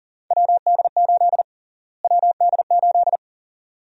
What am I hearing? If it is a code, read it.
Morse code, heard as a keyed tone, about 40 wpm, 700 Hz. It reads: WD8 WD8
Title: USA Call Sign Prefixes - Repeated 2x - 40wpm